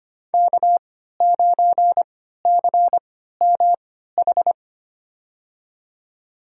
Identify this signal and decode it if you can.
Morse code, heard as a keyed tone, about 25 wpm, spaced as for 14 wpm, 700 Hz. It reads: K9CMH
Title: USA Call Signs - 5 Characters - 3x Character Spacing - 25wpm